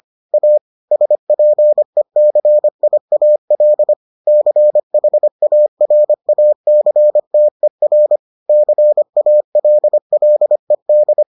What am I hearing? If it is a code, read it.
Morse code, heard as a keyed tone, about 25 wpm, 600 Hz. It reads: A SPECIAL CHARACTER CALLED